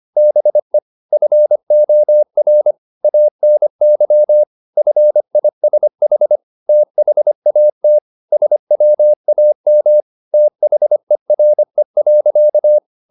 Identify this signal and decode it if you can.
Morse code, heard as a keyed tone, about 25 wpm, 600 Hz. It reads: BE FOR ANY FISH THAT SWAM THERE.
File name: the old man and the sea part 1 25wpm